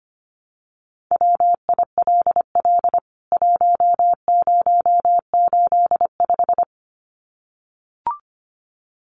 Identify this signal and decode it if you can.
Morse code, heard as a keyed tone, about 25 wpm, 690 Hz. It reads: WILL 1085
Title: CWT Exchange 25wpm